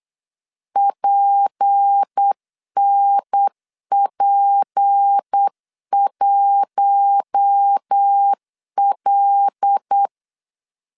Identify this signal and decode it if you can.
Morse code, heard as a keyed tone, about 8 wpm, 800 Hz. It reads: PNP1L
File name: CW / DET / 1 6 1500C S C 8wpm 5219